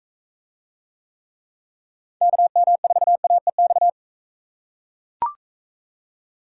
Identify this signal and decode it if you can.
Morse code, heard as a keyed tone, about 43 wpm, 700 Hz. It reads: KM4AEX